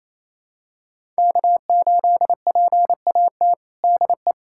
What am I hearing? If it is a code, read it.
Morse code, heard as a keyed tone, about 27 wpm, 700 Hz. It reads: K8PAT DE
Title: USA Callsign DE Callsign - 5 Characters - 28wpm